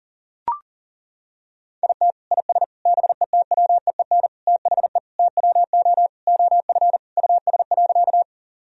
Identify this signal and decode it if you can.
Morse code, heard as a keyed tone, about 40 wpm, 700 Hz. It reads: IT IS BETWEEN THE TWO OF US.
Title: Sentences from Top 1000 Words - Review Part 1 - 40wpm